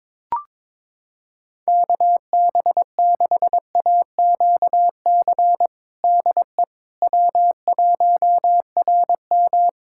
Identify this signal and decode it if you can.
Morse code, heard as a keyed tone, about 22 wpm, 700 Hz. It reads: KB6AQC DE W1RM
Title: Practice QSO - Exchange 4 - Ending - 22wpm